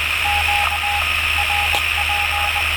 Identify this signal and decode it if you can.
Morse code, heard as a keyed tone, about 20 wpm, spaced as for 16 wpm, 820 Hz, with radio noise. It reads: QRP
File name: attachments / 27.07.2017 R4NX 6 QRP